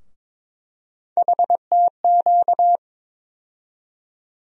Morse code, heard as a keyed tone, about 22 wpm, 700 Hz. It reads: HTQ
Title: USA Call Sign Postfixes - 3 Letters 22wpm